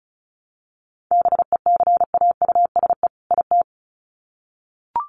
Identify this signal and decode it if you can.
Morse code, heard as a keyed tone, about 34 wpm, 700 Hz. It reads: BECAUSE IT
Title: Sets of 2 Words 35wpm